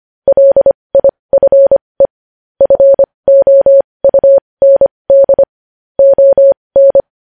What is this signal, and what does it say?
Morse code, heard as a keyed tone, about 25 wpm, spaced as for 20 wpm, 550 Hz. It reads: LIFE FOUND ON
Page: news-headlines-in-morse-code-at-20-wpm